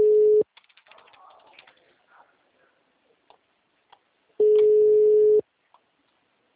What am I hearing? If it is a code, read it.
Ringback tone (Europe, 425 Hz).
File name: out- -2006- - - .0